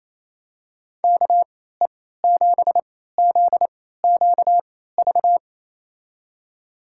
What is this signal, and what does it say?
Morse code, heard as a keyed tone, about 28 wpm, spaced as for 16 wpm, 700 Hz. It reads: KE7ZQV
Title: USA Call Signs - 3x Character Spacing - 28wpm